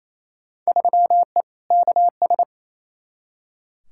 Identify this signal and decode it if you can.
Morse code, heard as a keyed tone, about 28 wpm, 700 Hz. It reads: 3E KS
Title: Field Day Exchanges Encore - 28wpm